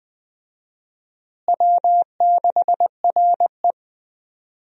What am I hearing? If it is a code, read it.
Morse code, heard as a keyed tone, about 20 wpm, 700 Hz. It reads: W6RE
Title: USA Call Signs - 4 Characters - Repeated 3x - 20wpm